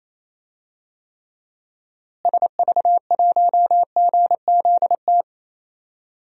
Morse code, heard as a keyed tone, about 28 wpm, 700 Hz. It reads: SV1GZT
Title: DXCC-100 Call Signs - 6 Characters - Repeated 3x - 28wpm